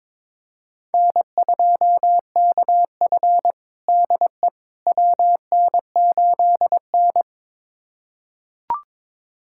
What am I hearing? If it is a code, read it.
Morse code, heard as a keyed tone, about 22 wpm, 700 Hz. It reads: N2KF DE WN8N
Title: USA Callsign DE Callsign - 4 Characters - 22wpm